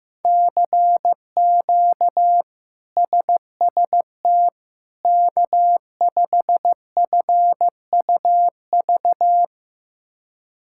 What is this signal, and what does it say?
Morse code, heard as a keyed tone, about 15 wpm, 700 Hz. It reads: CQ SST K5FUV
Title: K1USN SST QSO Practice - 15wpm